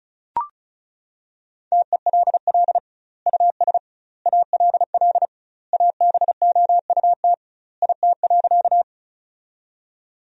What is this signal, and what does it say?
Morse code, heard as a keyed tone, about 34 wpm, 700 Hz. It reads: TELL US ALL ABOUT IT.
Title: 2x Word-Spacing - Sentences from Top 200 Words - 35wpm